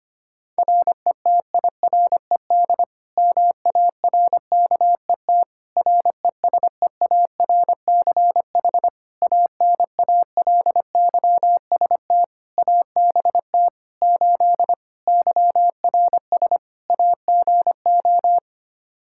Morse code, heard as a keyed tone, about 25 wpm, 700 Hz. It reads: RETIRED MARKET RESEARCH ANALYST ABT 8 YRS AGO
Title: Practice QSO - Exchange 3 - More Info - 25wpm